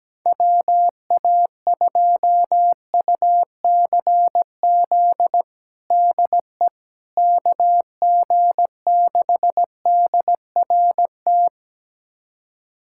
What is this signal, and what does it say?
Morse code, heard as a keyed tone, about 17 wpm, 700 Hz. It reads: WA2UCZ DE KG6DRT
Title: USA Callsign DE Callsign - 6 Characters - 17wpm